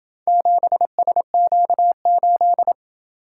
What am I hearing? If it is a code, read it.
Morse code, heard as a keyed tone, about 26 wpm, 700 Hz. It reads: 7SQ8